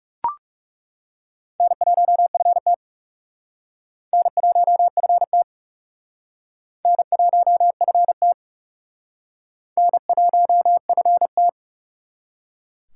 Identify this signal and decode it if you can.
Morse code, heard as a keyed tone, about 37 wpm, 700 Hz. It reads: N1FT N1FT N1FT N1FT
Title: Speed Racing - USA 1x2 Call Signs 30wpm